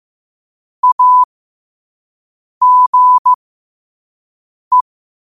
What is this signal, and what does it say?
Morse code, heard as a keyed tone, about 15 wpm, spaced as for 5 wpm, 1000 Hz. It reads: AGE
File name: qotdfiles / 5wpm